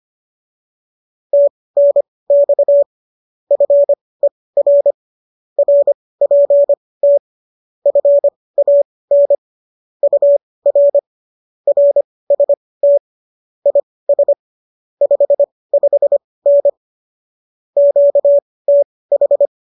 Morse code, heard as a keyed tone, about 25 wpm, spaced as for 18 wpm, 570 Hz. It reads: TNX FER RPT FAN UR RST IS 55N QTH